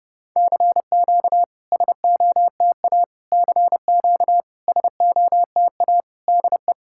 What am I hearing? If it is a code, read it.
Morse code, heard as a keyed tone, about 32 wpm, 690 Hz. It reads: CQ SOTA CQ SOTA DE
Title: SOTA QSO Practice 30wpm